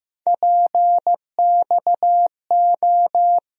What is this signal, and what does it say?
Morse code, heard as a keyed tone, about 15 wpm, 700 Hz. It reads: PXO
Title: USA Call Sign Postfixes - 3 Letters 15wpm